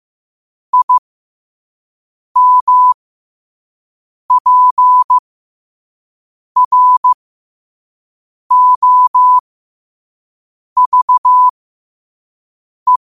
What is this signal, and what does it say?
Morse code, heard as a keyed tone, about 15 wpm, spaced as for 5 wpm, 1000 Hz. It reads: IMPROVE